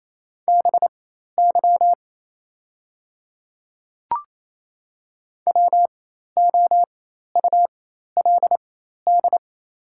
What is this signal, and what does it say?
Morse code, heard as a keyed tone, about 27 wpm, spaced as for 13 wpm, 700 Hz. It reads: BY WOULD
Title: 4x Character Spacing - Top 100 Words - 28wpm